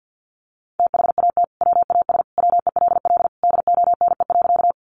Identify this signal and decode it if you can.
Morse code, heard as a keyed tone, about 50 wpm, 700 Hz. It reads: THAT WAS WELL DONE.
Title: Sentences from Top 500 Words 50wpm